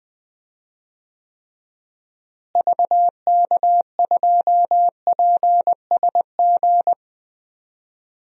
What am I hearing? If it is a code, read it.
Morse code, heard as a keyed tone, about 20 wpm, 700 Hz. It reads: VK2PSG
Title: DXCC-100 Call Signs - 6 Characters - Repeated 2x - 20wpm